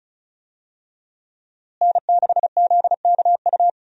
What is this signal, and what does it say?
Morse code, heard as a keyed tone, about 35 wpm, 700 Hz. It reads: N6ZKU
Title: USA Call Signs - 5 Characters - Repeated 2x - 35wpm